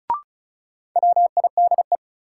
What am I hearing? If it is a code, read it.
Morse code, heard as a keyed tone, about 36 wpm, 700 Hz. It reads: WIDE
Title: Sets of 4 Words - New Word Drill - 35wpm